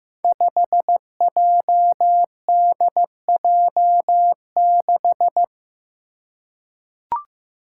Morse code, heard as a keyed tone, about 15 wpm, 700 Hz. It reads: 5JDJ6
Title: Five Challenging Letter-Number Combinations 15wpm